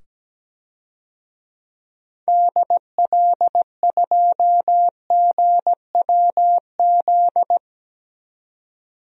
Morse code, heard as a keyed tone, about 17 wpm, 700 Hz. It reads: DL2GWZ
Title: DXCC-100 Call Signs - 6 Characters - 17wpm